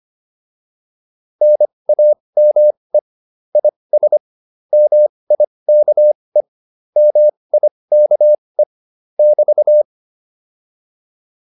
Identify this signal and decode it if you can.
Morse code, heard as a keyed tone, about 25 wpm, spaced as for 20 wpm, 600 Hz. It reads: NAME IS MIKE MIKE <BT>
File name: QSO310a 20